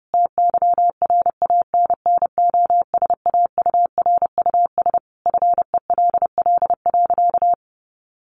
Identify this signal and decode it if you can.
Morse code, heard as a keyed tone, about 30 wpm, 700 Hz. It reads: TYRANNOSAURUS FELL.
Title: A Sound Of Thunder 30wpm